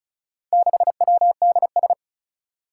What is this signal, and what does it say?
Morse code, heard as a keyed tone, about 36 wpm, 700 Hz. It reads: 6WDS
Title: Four Challenging Letter-Number Combinations 35wpm